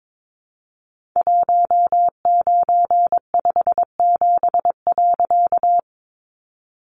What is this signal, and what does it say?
Morse code, heard as a keyed tone, about 22 wpm, 700 Hz. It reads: 1957.
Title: A Sound Of Thunder 22wpm